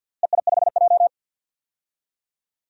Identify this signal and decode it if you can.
Morse code, heard as a keyed tone, about 50 wpm, 700 Hz. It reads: EI5J
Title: DXCC-100 Call Signs - 4 Characters - Repeated 2x - 50wpm